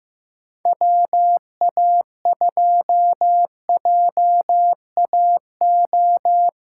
Morse code, heard as a keyed tone, about 14 wpm, 700 Hz. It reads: WA2JAO
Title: USA Callsign DE Callsign 15wpm